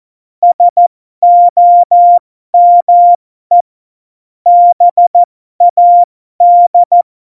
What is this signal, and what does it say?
Morse code, heard as a keyed tone, about 14 wpm, spaced as for 12 wpm, 700 Hz. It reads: SOME BAD
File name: Morsekurs / Witze